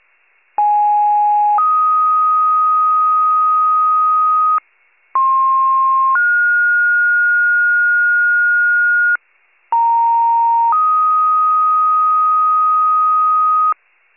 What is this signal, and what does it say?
Two-tone page, heard as a one-second tone, then a three-second tone, about 830 then 1250 Hz, about 1060 then 1430 Hz, about 930 then 1250 Hz. It is heard from 0.6 to 13.7 s.
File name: 94 FIRE